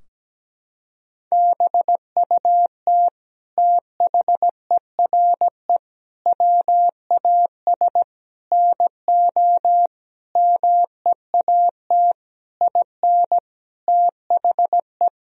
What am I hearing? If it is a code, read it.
Morse code, heard as a keyed tone, about 17 wpm, 700 Hz. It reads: BUT THERE WAS NO MEAT IN THE
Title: Sentences from Top 1000 Words - Review Part 3 - 17wpm